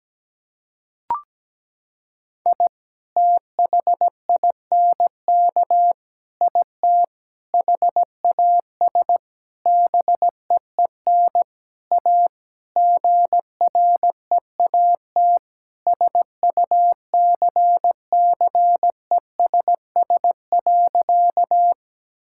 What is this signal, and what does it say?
Morse code, heard as a keyed tone, about 17 wpm, 700 Hz. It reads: I THINK IT HAS BEEN A GREAT SUCCESS.
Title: Sentences from Top 1000 Words - Review Part 6 - 17wpm